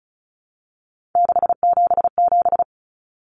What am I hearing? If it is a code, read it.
Morse code, heard as a keyed tone, about 34 wpm, 700 Hz. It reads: 677
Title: Numbers - 3-Digit 35wpm